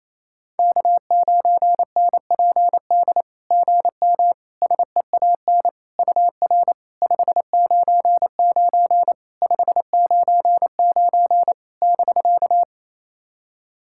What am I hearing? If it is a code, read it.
Morse code, heard as a keyed tone, about 28 wpm, 700 Hz. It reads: K9NPD GM SEAN UR 599 599 <BK>
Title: SOTA QSO Practice 28wpm